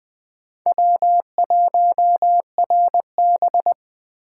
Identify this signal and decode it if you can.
Morse code, heard as a keyed tone, about 20 wpm, 700 Hz. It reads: W1RB
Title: Speed Racing - USA 1x2 Call Signs 17wpm